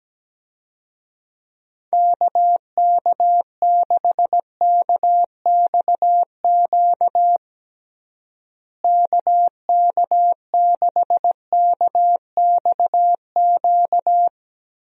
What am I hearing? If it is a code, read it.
Morse code, heard as a keyed tone, about 17 wpm, 700 Hz. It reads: KK6KXQ KK6KXQ
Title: USA Call Signs - 6 Characters - Repeated 2x - 17wpm